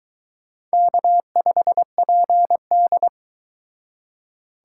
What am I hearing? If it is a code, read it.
Morse code, heard as a keyed tone, about 22 wpm, 700 Hz. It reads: K5PD